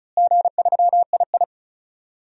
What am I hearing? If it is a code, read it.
Morse code, heard as a keyed tone, about 36 wpm, 700 Hz. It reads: G3II
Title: DXCC-100 Call Signs - 4 Characters - Repeated 2x - 35wpm